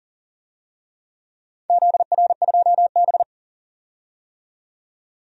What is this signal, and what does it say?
Morse code, heard as a keyed tone, about 38 wpm, 700 Hz. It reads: ZR2B
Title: DXCC-100 Call Signs - 4 Characters - 40wpm